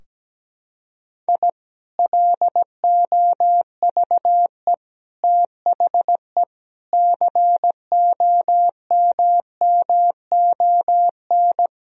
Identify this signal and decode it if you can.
Morse code, heard as a keyed tone, about 17 wpm, 700 Hz. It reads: I LOVE THE COMMON
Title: Sentences from Top 1000 Words - Review Part 3 - 17wpm